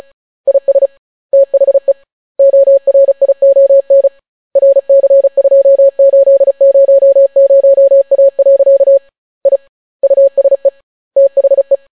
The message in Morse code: IS THE ORION RC2800A. I USE THE

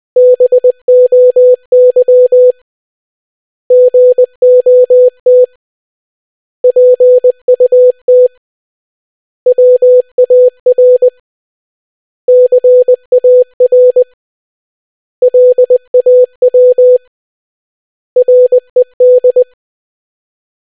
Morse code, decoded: BOY GOT PUT WAR CAR LAW RED

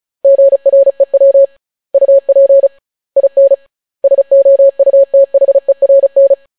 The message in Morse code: GREW UP IN SOUTHERN

